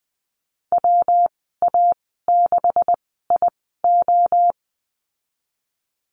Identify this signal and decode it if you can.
Morse code, heard as a keyed tone, about 20 wpm, spaced as for 14 wpm, 700 Hz. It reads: WA6IO
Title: 128 Lesson 21g - Number 6 - Call Signs 20wpm